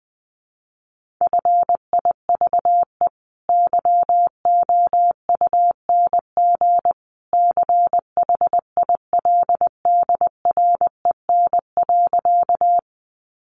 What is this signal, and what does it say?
Morse code, heard as a keyed tone, about 20 wpm, 700 Hz. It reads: FIVE YOUNG CHILDREN.
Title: Sentences from Top 400 Words 20wpm